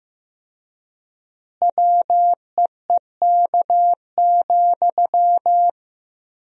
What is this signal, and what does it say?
Morse code, heard as a keyed tone, about 15 wpm, 700 Hz. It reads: WEEK,
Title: 4x Word-Spacing - Sentences from Top 500 Words - 15wpm